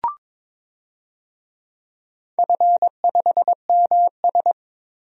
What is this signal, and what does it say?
Morse code, heard as a keyed tone, about 21 wpm, 700 Hz. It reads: F5MS
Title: DXCC-100 Call Signs - 4 Characters - 22wpm